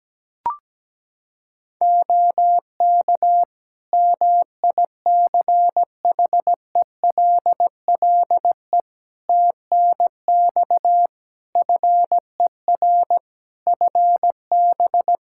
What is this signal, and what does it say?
Morse code, heard as a keyed tone, about 17 wpm, 700 Hz. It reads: OK MICHELLE TNX FER FB
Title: Practice QSO - Exchange 4 - Ending - 17wpm